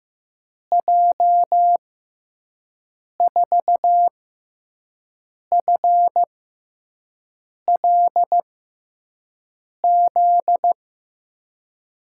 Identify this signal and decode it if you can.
Morse code, heard as a keyed tone, about 15 wpm, spaced as for 5 wpm, 700 Hz. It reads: J4FLZ
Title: USA Call Signs - 6x Character Spacing - 15wpm